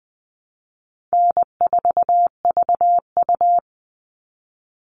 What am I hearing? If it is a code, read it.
Morse code, heard as a keyed tone, about 20 wpm, 700 Hz. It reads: N4VU
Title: All Cut Numbers - 4-Digit 20wpm